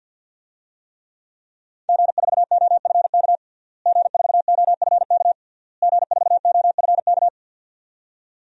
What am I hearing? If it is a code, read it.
Morse code, heard as a keyed tone, about 52 wpm, 700 Hz. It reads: G4OFK G4OFK G4OFK
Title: DXCC-100 Call Signs - 5 Characters - Repeated 3x - 50wpm